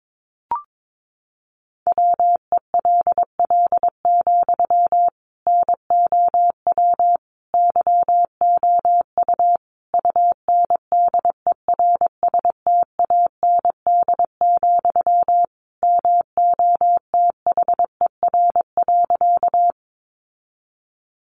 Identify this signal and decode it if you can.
Morse code, heard as a keyed tone, about 22 wpm, 700 Hz. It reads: WELL, NOW YOU UNDERSTAND, MOTHER.
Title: Sentences from Top 600 Words 22wpm